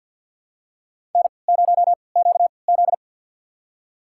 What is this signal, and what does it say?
Morse code, heard as a keyed tone, about 46 wpm, spaced as for 27 wpm, 700 Hz. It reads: N0QZ
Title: USA Call Signs - 4 Characters - 3x Character Spacing - 50wpm